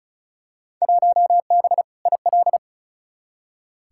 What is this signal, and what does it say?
Morse code, heard as a keyed tone, about 36 wpm, 700 Hz. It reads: 1B IL
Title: Field Day Exchanges - 35wpm